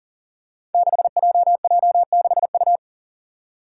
Morse code, heard as a keyed tone, about 43 wpm, 700 Hz. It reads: 6JJ6U
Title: Five Challenging Letter-Number Combinations 40wpm